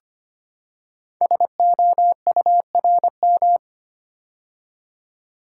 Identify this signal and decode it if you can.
Morse code, heard as a keyed tone, about 25 wpm, 700 Hz. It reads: SOURM
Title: Code Groups - 5 Letters - 25wpm